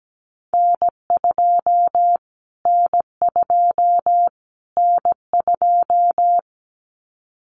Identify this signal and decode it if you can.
Morse code, heard as a keyed tone, about 17 wpm, 700 Hz. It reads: N2 N2 N2
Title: Callsign Prefixes for Top 25 DXCC - Repeated 3x - 17wpm